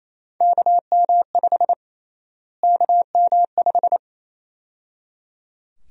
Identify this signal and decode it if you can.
Morse code, heard as a keyed tone, about 28 wpm, 700 Hz. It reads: KM5 KM5
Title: USA Call Sign Prefixes - Repeated 2x - 28wpm